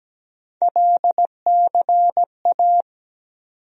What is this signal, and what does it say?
Morse code, heard as a keyed tone, about 17 wpm, 700 Hz. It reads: LCA
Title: USA Call Sign Postfixes - 3 Letters 17wpm